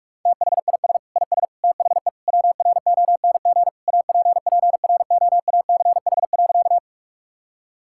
Morse code, heard as a keyed tone, about 41 wpm, 700 Hz. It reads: THIS IS THE WRONG APPROACH.
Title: Sentences from Top 1000 Words - Review Part 6 - 45wpm